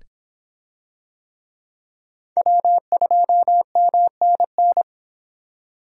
Morse code, heard as a keyed tone, about 23 wpm, 700 Hz. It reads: W2MNN